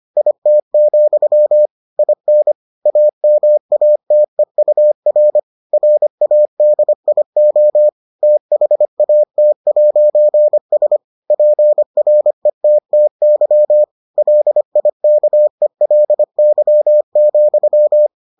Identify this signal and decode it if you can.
Morse code, heard as a keyed tone, about 25 wpm, 600 Hz. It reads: IT, IN AMATEUR RADIO THAT'S PRETTY LIKELY,